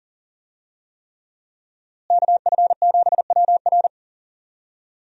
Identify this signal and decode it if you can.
Morse code, heard as a keyed tone, about 40 wpm, 700 Hz. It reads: KF7WR
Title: USA Call Signs - 5 Characters - Repeated 3x - 40wpm